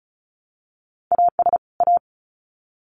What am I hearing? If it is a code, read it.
Morse code, heard as a keyed tone, about 34 wpm, 700 Hz. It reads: AS A
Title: Sets of 2 Words 35wpm